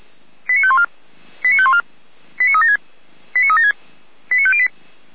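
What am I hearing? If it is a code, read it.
Five-tone selective call (ZVEI), heard as quick runs of five tones, tones of about 70 ms. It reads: 89514 89514 89376 89376 89508
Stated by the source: ZVEI five-tone selective call 89079, received by SDR.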